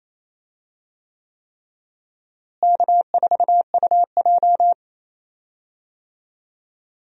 Morse code, heard as a keyed tone, about 28 wpm, 700 Hz. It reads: K4UJ
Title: USA Call Signs - 4 Characters - Repeated 3x - 28wpm